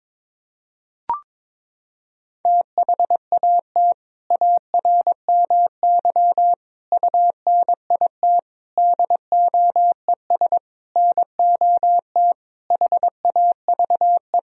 Morse code, heard as a keyed tone, about 22 wpm, 700 Hz. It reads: THAT ARMY UNIT DOES NOT HAVE
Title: Sentences from Top 1000 Words - Review Part 2 - 22wpm